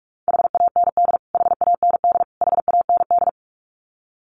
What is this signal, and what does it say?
Morse code, heard as a keyed tone, about 44 wpm, 700 Hz. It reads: HAND HAND HAND
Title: Top 200 Words Repeated 45wpm